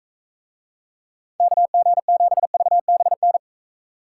Morse code, heard as a keyed tone, about 41 wpm, 700 Hz. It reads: KG7VBN